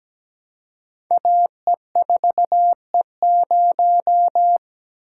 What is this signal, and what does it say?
Morse code, heard as a keyed tone, about 17 wpm, 700 Hz. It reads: AE4E0
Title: Code Groups - 5 Letter:Numbers - 17wpm